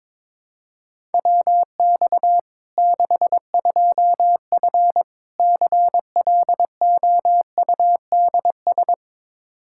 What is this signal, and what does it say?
Morse code, heard as a keyed tone, about 22 wpm, 700 Hz. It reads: WX 62F CLOUDS
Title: K1USN SST QSO Practice 22wpm